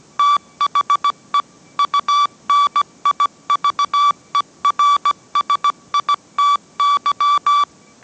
Morse code, heard as a keyed tone, about 17 wpm, 1150 Hz. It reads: THE UNIVERSITY